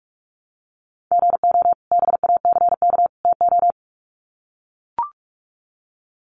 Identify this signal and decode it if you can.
Morse code, heard as a keyed tone, about 44 wpm, 700 Hz. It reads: GO BACK TO
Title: Sets of 3 Words 45wpm